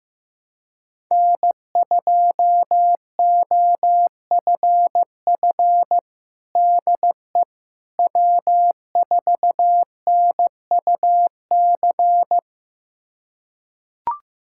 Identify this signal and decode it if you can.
Morse code, heard as a keyed tone, about 15 wpm, 700 Hz. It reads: N2OFF DE W4NUC
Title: USA Callsign DE Callsign - 5 Characters - 15wpm